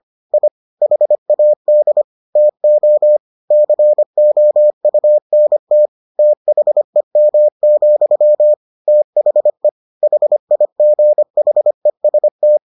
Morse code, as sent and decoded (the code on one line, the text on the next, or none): .. / .... .- -.. / - --- / -.-. --- ..- -. - / - .... -- --..-- / - .... / .... .. --. .... ... -
I HAD TO COUNT THEM, THE HIGHEST